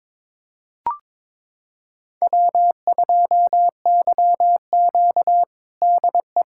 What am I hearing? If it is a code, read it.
Morse code, heard as a keyed tone, about 22 wpm, 700 Hz. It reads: W2YQ DE
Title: USA Callsign DE Callsign - 4 Characters - 22wpm